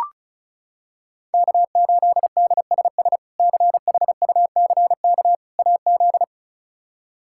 Morse code, heard as a keyed tone, about 36 wpm, 700 Hz. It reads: K8DSS CHUCK AZ
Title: K1USN SST QSO Practice - 35wpm